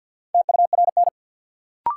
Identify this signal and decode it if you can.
Morse code, heard as a keyed tone, about 50 wpm, 700 Hz. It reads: TURN
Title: Sentences from Top 200 Words - New Word Drill - 50wpm